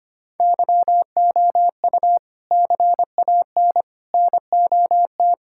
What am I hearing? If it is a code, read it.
Morse code, heard as a keyed tone, about 25 wpm, 700 Hz. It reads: YOU CAN NOT
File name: Sentences with 3-Word Phrases - 25wpm